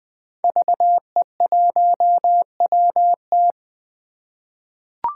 Morse code, heard as a keyed tone, about 20 wpm, 700 Hz. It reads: VE1WT
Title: 2023 SOTA Chaser Call Signs - 20wpm